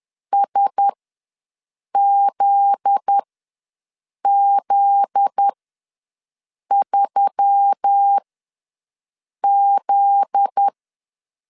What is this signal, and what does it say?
Morse code, heard as a keyed tone, about 10 wpm, spaced as for 6 wpm, 800 Hz. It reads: SZZ3Z
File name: LEZIONI / 7 GF9Z / 7 GF9Z ws 10wpm 3919